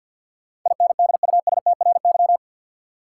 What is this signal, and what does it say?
Morse code, heard as a keyed tone, about 52 wpm, 700 Hz. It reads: INDUSTRY
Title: Sentences from Top 1000 Words - Review Part 4 - New Word Drill - 50wpm